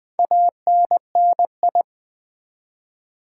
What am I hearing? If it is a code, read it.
Morse code, heard as a keyed tone, about 20 wpm, 700 Hz. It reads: ANNI
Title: Build A Word - 4 Letters At A Time - 11-Letter Words - 20wpm